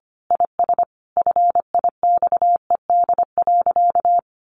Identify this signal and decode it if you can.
Morse code, heard as a keyed tone, about 25 wpm, 700 Hz. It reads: IS FIXED.
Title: Sentences from Top 1100 Words 25wpm